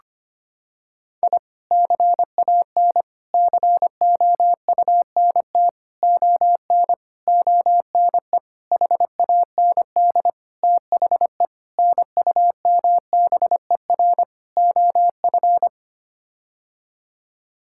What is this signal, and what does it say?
Morse code, heard as a keyed tone, about 25 wpm, 700 Hz. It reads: I CAN COUNT ON ONE HAND THE NUMBER OF
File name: Sets of 9 Words - 25wpm